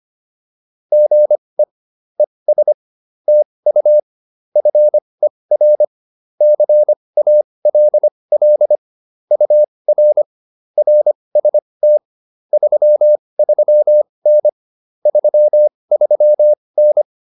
Morse code, heard as a keyed tone, about 25 wpm, spaced as for 20 wpm, 600 Hz. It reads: GE ES TU FER CALL UR RST 33N 33N
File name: QSO105a 20